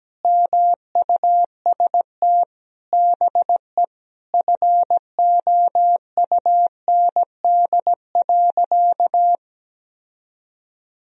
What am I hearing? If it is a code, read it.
Morse code, heard as a keyed tone, about 17 wpm, 700 Hz. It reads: MUST BE FOUND.
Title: Sentences from Top 1000 Words - Review Part 4 - 17wpm